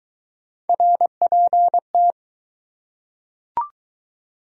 Morse code, heard as a keyed tone, about 19 wpm, 700 Hz. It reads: RPT
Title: Speed-Racing Abbreviations - 15wpm